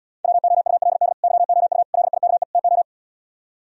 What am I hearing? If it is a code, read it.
Morse code, heard as a keyed tone, about 77 wpm, 700 Hz. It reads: WOULD YOU LIKE TO